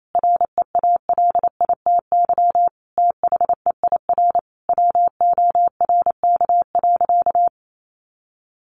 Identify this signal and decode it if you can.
Morse code, heard as a keyed tone, about 28 wpm, 700 Hz. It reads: REALITY THEIR WORK.
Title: Sentences from Top 1400 Words 28wpm